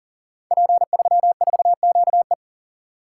Morse code, heard as a keyed tone, about 39 wpm, 700 Hz. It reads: P34QE
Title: Code Groups - 5 Letter:Numbers - 40wpm